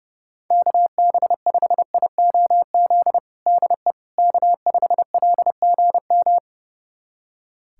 Morse code, heard as a keyed tone, about 32 wpm, 700 Hz. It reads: KB5IOZ DE K5LGM
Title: USA Callsign DE Callsign 30wpm